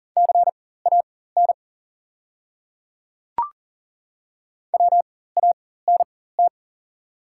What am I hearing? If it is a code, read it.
Morse code, heard as a keyed tone, about 38 wpm, spaced as for 18 wpm, 690 Hz. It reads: CAN WANT
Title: 4x Character Spacing - Top 300 Words - 40wpm